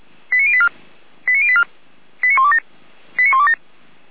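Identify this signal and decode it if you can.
Five-tone selective call (ZVEI), heard as quick runs of five tones, tones of about 70 ms. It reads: 89084 89084 89127 89127
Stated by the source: ZVEI five-tone selective call received by SDR, address 89354.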